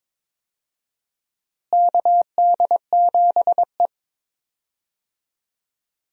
Morse code, heard as a keyed tone, about 22 wpm, 700 Hz. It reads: KD7E
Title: USA Call Signs - 4 Characters - Repeated 2x - 22wpm